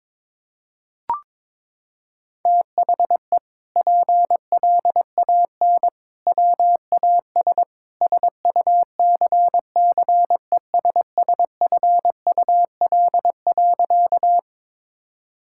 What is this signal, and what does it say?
Morse code, heard as a keyed tone, about 22 wpm, 700 Hz. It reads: THE PLAN WAS SUCCESSFUL.
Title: Sentences from Top 1000 Words - Review Part 6 - 22wpm